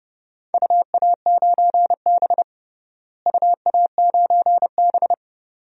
Morse code, heard as a keyed tone, about 31 wpm, 700 Hz. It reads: UA9B UA9B
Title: DXCC-100 Call Signs - 4 Characters - Repeated 2x - 30wpm